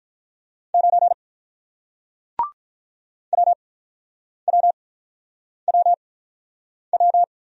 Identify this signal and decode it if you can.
Morse code, heard as a keyed tone, about 45 wpm, spaced as for 9 wpm, 700 Hz. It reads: 9 WWWW